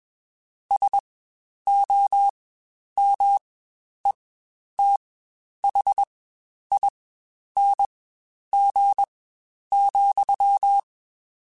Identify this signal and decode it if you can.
Morse code, heard as a keyed tone, about 21 wpm, spaced as for 10 wpm, 800 Hz. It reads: SOMETHING,
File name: Quotes Spaced / Quotes Spaced-mp3s / StandFor sp @20